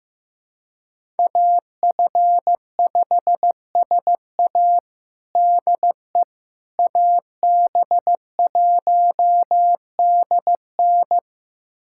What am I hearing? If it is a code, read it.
Morse code, heard as a keyed tone, about 16 wpm, 700 Hz. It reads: AF5SA DE AB1DN
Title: USA Callsign DE Callsign - 5 Characters - 15wpm